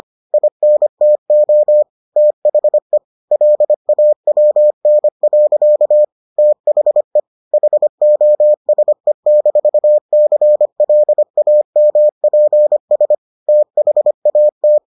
Morse code, heard as a keyed tone, about 25 wpm, 600 Hz. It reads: INTO THE LAWN. THE HOSE-CLAMPS THAT